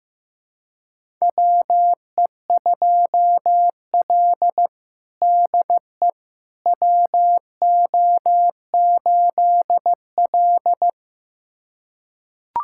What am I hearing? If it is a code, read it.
Morse code, heard as a keyed tone, about 15 wpm, 700 Hz. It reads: WE2L DE WO8L
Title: USA Callsign DE Callsign - 4 Characters - 15wpm